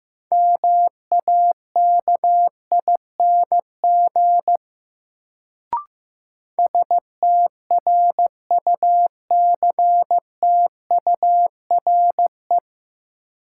Morse code, heard as a keyed tone, about 16 wpm, 700 Hz. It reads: MAKING STRUCTURE
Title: Sentences from Top 1000 Words - Review Part 6 - New Word Drill - 15wpm